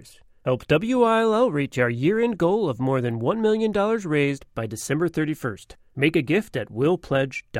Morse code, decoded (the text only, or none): none